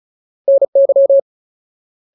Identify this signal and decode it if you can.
Morse code, heard as a keyed tone, about 34 wpm, 550 Hz. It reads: NY
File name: Ultimate Word List / 35WPM Ultimate